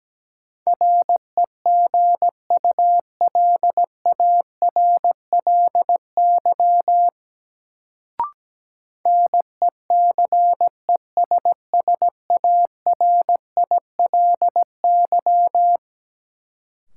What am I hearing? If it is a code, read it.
Morse code, heard as a keyed tone, about 17 wpm, 700 Hz. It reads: REGULARLY NECESSARILY
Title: Sentences from Top 1400 Words - New Word Drill Alt - 17wpm